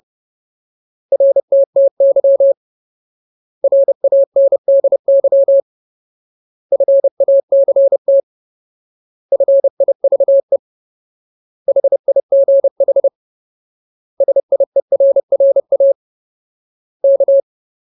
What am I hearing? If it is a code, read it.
Morse code, heard as a keyed tone, about 30 wpm, 550 Hz. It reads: RTTY RANDY FACT FIVE HIGH SIERRA K
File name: Ultimate Word List / 30WPM Ultimate